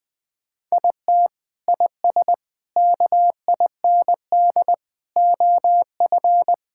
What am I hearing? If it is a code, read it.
Morse code, heard as a keyed tone, about 20 wpm, 700 Hz. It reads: IT IS KIND OF
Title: Sentences with 2-Word Phrases - 20wpm